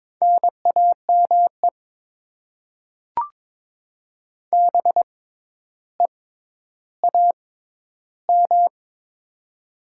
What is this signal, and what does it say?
Morse code, heard as a keyed tone, about 21 wpm, spaced as for 11 wpm, 700 Hz. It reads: NAME BEAM
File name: Top 90 QSO Elements - 6x Character Spacing - 22wpm